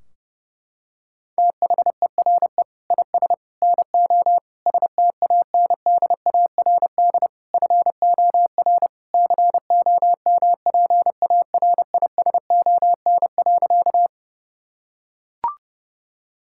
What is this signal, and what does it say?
Morse code, heard as a keyed tone, about 30 wpm, 700 Hz. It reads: THERE IS NO STANDARD FOR COMPARISON.
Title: Sentences from Top 1600 Words 30wpm